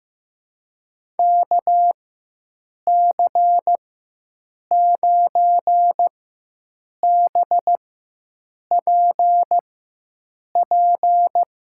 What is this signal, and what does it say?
Morse code, heard as a keyed tone, about 15 wpm, spaced as for 7 wpm, 700 Hz. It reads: KC9BPP
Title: USA Call Signs - 6 Characters - 4x Character Spacing - 15wpm